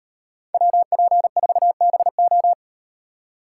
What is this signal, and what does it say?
Morse code, heard as a keyed tone, about 35 wpm, 700 Hz. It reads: WP4BO